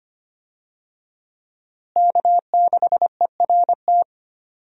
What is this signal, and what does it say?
Morse code, heard as a keyed tone, about 25 wpm, 700 Hz. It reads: K6ERT
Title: USA Call Signs - 5 Characters - Repeated 3x - 25wpm